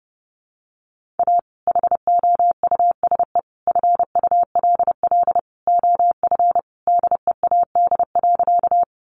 Morse code, heard as a keyed tone, about 30 wpm, 700 Hz. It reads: A HOUSE FULL OF DEAD.
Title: Sentences from Top 1500 Words 30wpm